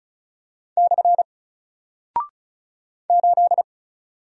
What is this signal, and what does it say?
Morse code, heard as a keyed tone, about 36 wpm, 700 Hz. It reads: / 8
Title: Single Character (Letter:Number:Punctuation:Prosign) 35wpm